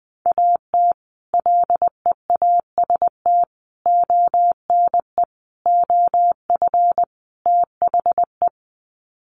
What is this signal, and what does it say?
Morse code, heard as a keyed tone, about 20 wpm, 700 Hz. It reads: AT LEAST ONE OF THE
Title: Sets of 5 Words Encore 20wpm